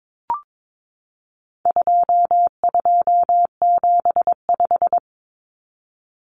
Morse code, heard as a keyed tone, about 22 wpm, 700 Hz. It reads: 2275
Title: Common Cut Numbers - 4-Digit 22wpm